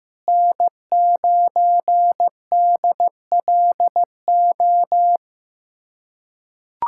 Morse code, heard as a keyed tone, about 15 wpm, 700 Hz. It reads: N9DLO
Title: USA Callsign DE Callsign - 5 Characters - 15wpm